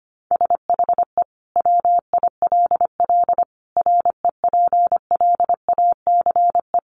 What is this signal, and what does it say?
Morse code, heard as a keyed tone, about 25 wpm, 700 Hz. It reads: SHE WILL REPLACE
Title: Sentences from Top 1400 Words 25wpm